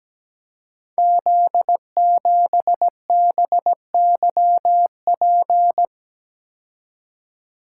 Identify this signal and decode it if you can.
Morse code, heard as a keyed tone, about 17 wpm, 700 Hz. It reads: Z7BYP
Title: Code Groups - 5 Letter:Numbers - 17wpm